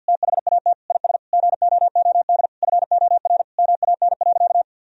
Morse code, heard as a keyed tone, about 52 wpm, 700 Hz. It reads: THAT IS GOOD FOR MAN.
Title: Sentences from Top 1000 Words - Review Part 3 - 50wpm